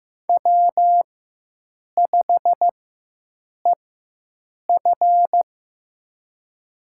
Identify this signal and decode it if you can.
Morse code, heard as a keyed tone, about 15 wpm, spaced as for 7 wpm, 700 Hz. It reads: W5EF
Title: USA Call Signs - 4x Character Spacing - 15wpm